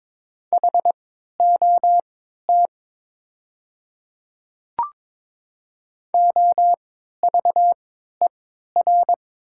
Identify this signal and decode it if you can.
Morse code, heard as a keyed tone, about 22 wpm, spaced as for 13 wpm, 700 Hz. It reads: HOT OVER
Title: 3x Character Spacing - Top 300 Words - 22wpm